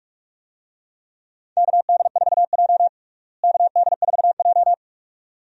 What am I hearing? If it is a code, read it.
Morse code, heard as a keyed tone, about 48 wpm, 700 Hz. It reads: KD4J KD4J
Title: USA Call Signs - 4 Characters - Repeated 2x - 45wpm